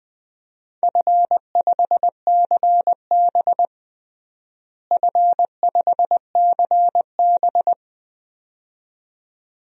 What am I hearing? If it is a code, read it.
Morse code, heard as a keyed tone, about 20 wpm, 700 Hz. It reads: F5CB F5CB
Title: DXCC-100 Call Signs - 4 Characters - Repeated 2x - 20wpm